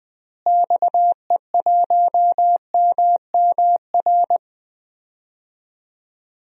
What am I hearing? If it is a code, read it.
Morse code, heard as a keyed tone, about 20 wpm, 700 Hz. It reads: XE1MMR
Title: DXCC-100 Call Signs - 6 Characters - Repeated 3x - 20wpm